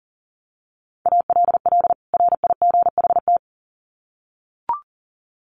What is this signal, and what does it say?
Morse code, heard as a keyed tone, about 40 wpm, 700 Hz. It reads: ALL RIGHT
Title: Sets of 2 Words 40wpm